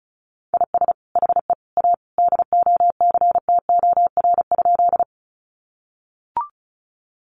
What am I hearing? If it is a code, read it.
Morse code, heard as a keyed tone, about 34 wpm, 700 Hz. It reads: IS HE A DOCTOR?
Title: Sentences from Top 1500 Words 35wpm